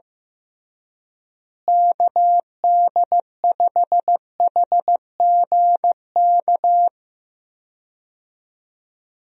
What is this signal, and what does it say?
Morse code, heard as a keyed tone, about 15 wpm, 700 Hz. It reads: KD5HGK
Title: USA Call Signs - 6 Characters - Repeated 3x - 15wpm